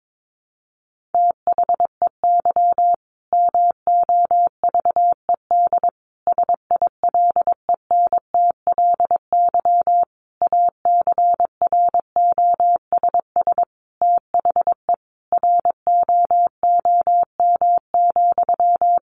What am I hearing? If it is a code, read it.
Morse code, heard as a keyed tone, about 22 wpm, 700 Hz. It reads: THEY MOVED SILENTLY ACROSS THE ROOM,
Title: A Sound Of Thunder 22wpm